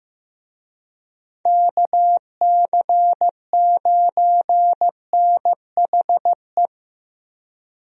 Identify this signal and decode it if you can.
Morse code, heard as a keyed tone, about 15 wpm, 700 Hz. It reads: KC9NHE
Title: USA Call Signs - 6 Characters - Repeated 3x - 15wpm